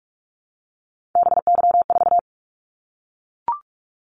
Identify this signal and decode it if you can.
Morse code, heard as a keyed tone, about 44 wpm, 700 Hz. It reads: BY4
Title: Callsign Prefixes for Top 100 DXCC - 45wpm